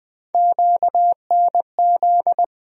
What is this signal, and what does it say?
Morse code, heard as a keyed tone, about 20 wpm, 700 Hz. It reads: QNZ
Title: USA Call Sign Postfixes - 3 Letters 20wpm